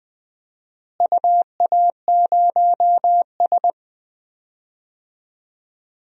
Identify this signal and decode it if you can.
Morse code, heard as a keyed tone, about 20 wpm, 700 Hz. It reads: UA0S
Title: DXCC-100 Call Signs - 4 Characters - Repeated 3x - 20wpm